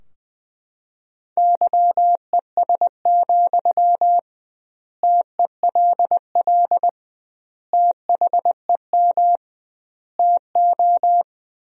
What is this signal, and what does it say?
Morse code, heard as a keyed tone, about 20 wpm, 700 Hz. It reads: YES, TELL THEM TO
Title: 2x Word-Spacing - Sentences from Top 500 Words - 20wpm